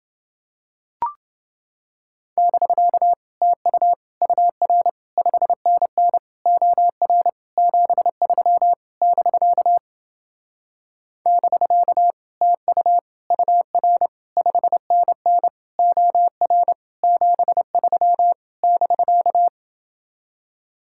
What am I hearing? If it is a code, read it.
Morse code, heard as a keyed tone, about 28 wpm, 700 Hz. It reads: <BK> TU UR 5NN OR 73 <BK> <BK> TU UR 5NN OR 73 <BK>